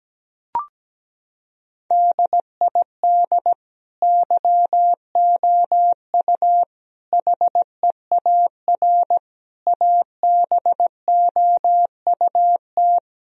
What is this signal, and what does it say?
Morse code, heard as a keyed tone, about 17 wpm, 700 Hz. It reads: DID YOU HEAR ABOUT
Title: Sentences from Top 1000 Words - Review Part 4 - 17wpm